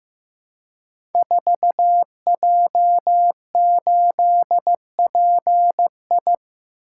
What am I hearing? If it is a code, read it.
Morse code, heard as a keyed tone, about 15 wpm, 700 Hz. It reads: 4J8PI
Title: Code Groups - 5 Letter:Numbers - 15wpm